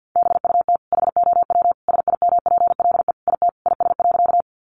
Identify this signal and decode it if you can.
Morse code, heard as a keyed tone, about 50 wpm, 700 Hz. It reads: BUT HOW SIMPLE IT IS.
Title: Sentences from Top 400 Words 50wpm